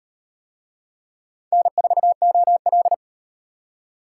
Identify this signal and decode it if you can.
Morse code, heard as a keyed tone, about 36 wpm, 700 Hz. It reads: N4OL